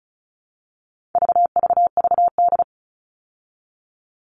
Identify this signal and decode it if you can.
Morse code, heard as a keyed tone, about 34 wpm, 700 Hz. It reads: VVVD
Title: All Cut Numbers - 4-Digit 35wpm